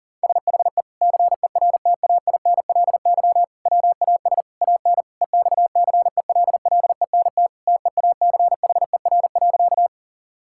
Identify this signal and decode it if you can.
Morse code, heard as a keyed tone, about 40 wpm, 700 Hz. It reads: SHE CERTAINLY WAS AN EXCELLENT TEACHER.
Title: Sentences from Top 1000 Words - Review Part 6 - 40wpm